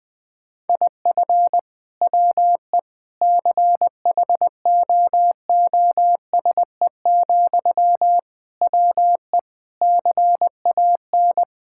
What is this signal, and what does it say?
Morse code, heard as a keyed tone, about 20 wpm, 700 Hz. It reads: IF WE CHOOSE, WE CAN
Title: Sentences from Top 1000 Words - Review Part 6 - 20wpm